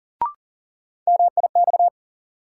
Morse code, heard as a keyed tone, about 40 wpm, 700 Hz. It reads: MIX
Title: Sentences from Top 1000 Words - Review Part 4 - New Word Drill - 40wpm